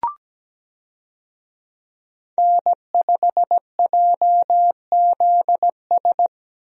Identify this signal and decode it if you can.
Morse code, heard as a keyed tone, about 17 wpm, 700 Hz. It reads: N5JZS
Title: USA Call Signs - 5 Characters - Repeated 2x - 17wpm